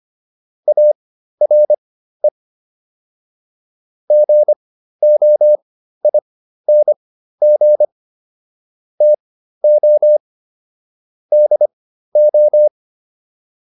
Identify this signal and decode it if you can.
Morse code, heard as a keyed tone, about 25 wpm, spaced as for 13 wpm, 600 Hz. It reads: ARE GOING TO DO